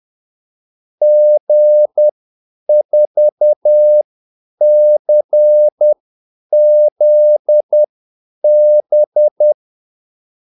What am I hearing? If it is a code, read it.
Morse code, heard as a keyed tone, about 10 wpm, spaced as for 8 wpm, 600 Hz. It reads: G4CZB
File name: Callsigns-10-8wpm